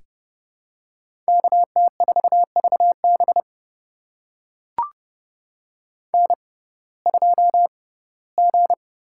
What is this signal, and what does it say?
Morse code, heard as a keyed tone, about 30 wpm, spaced as for 18 wpm, 700 Hz. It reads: KT4VB N2G
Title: USA Call Signs - 5 Characters - 6x Character Spacing - 30wpm